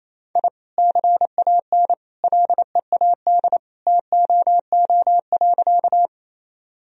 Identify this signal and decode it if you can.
Morse code, heard as a keyed tone, about 28 wpm, 700 Hz. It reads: I CAN LEAD TOO.
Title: Sentences from Top 1000 Words - Review Part 2 - 28wpm